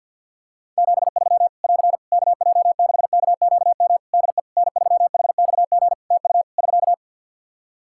Morse code, heard as a keyed tone, about 47 wpm, 700 Hz. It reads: 73 <AR> KJ6KQM DE N3HXG TU <SK>
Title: Practice QSO - Exchange 4 - Ending - 50wpm